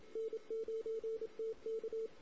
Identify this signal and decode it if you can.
Morse code, heard as a keyed tone, about 27 wpm, 450 Hz. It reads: N9TK
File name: CW / 2008-ss-cw-k5tr-op-n5rz -16:00